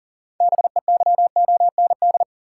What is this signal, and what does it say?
Morse code, heard as a keyed tone, about 38 wpm, 700 Hz. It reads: BEYOND
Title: Sentences from Top 1100 Words - New Word Drill Alt - 40wpm